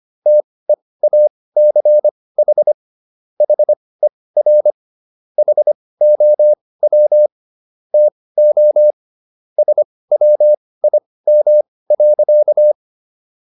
Morse code, heard as a keyed tone, about 25 wpm, spaced as for 18 wpm, 600 Hz. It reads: TEACH HER HOW TO SWIM.
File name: SS108 18